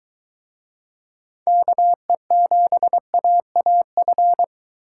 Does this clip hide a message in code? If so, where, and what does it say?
Yes: Morse code from 1.5 to 4.4 s, heard as a keyed tone, about 24 wpm, 700 Hz. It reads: KE7AAF